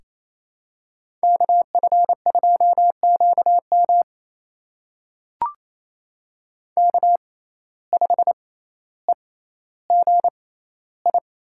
Morse code, heard as a keyed tone, about 28 wpm, spaced as for 14 wpm, 700 Hz. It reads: KF2QM K5EGI
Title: USA Call Signs - 5 Characters - 6x Character Spacing - 28wpm